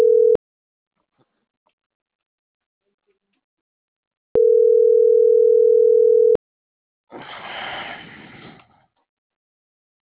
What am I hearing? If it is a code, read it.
Ringback tone (North America), 440 + 480 Hz.